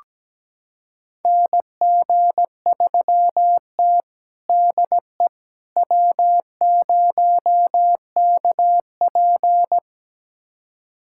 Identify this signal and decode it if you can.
Morse code, heard as a keyed tone, about 17 wpm, 700 Hz. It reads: NG3T DE W0KP
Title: USA Callsign DE Callsign - 4 Characters - 17wpm